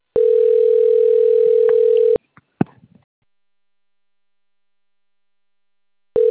Ringback tone (North America), 440 + 480 Hz.
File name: rg-1600-unknown- - - .5067